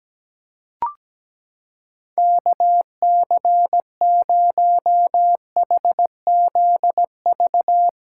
Morse code, heard as a keyed tone, about 17 wpm, 700 Hz. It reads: KC0HZV
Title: USA Callsign DE Callsign - 6 Characters - 17wpm